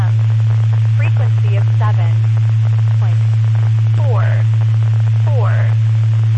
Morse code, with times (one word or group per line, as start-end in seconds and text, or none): none